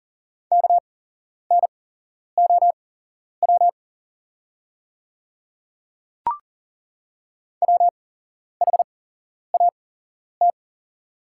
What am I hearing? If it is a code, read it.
Morse code, heard as a keyed tone, about 38 wpm, spaced as for 11 wpm, 700 Hz. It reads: KNOW WHAT
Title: 8x Character Spacing - Top 100 Words - 40wpm